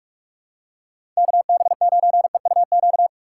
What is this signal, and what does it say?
Morse code, heard as a keyed tone, about 47 wpm, 700 Hz. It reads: KB9EUQ
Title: USA Call Signs - 4x Character Spacing - 45wpm